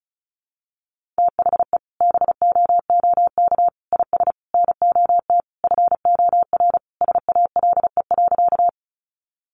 Morse code, heard as a keyed tone, about 34 wpm, 700 Hz. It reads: THE BOOK IS NOT FOR SALE.
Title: Sentences from Top 1400 Words 35wpm